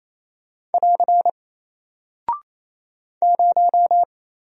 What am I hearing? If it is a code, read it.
Morse code, heard as a keyed tone, about 28 wpm, 700 Hz. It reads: <AR> 0
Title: Single Character (Letter:Number:Punctuation:Prosign) Alt 28wpm